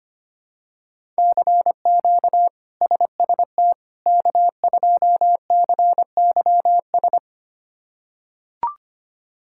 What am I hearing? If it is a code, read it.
Morse code, heard as a keyed tone, about 25 wpm, 700 Hz. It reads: CQ SST K2CYS
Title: K1USN SST QSO Practice - 25wpm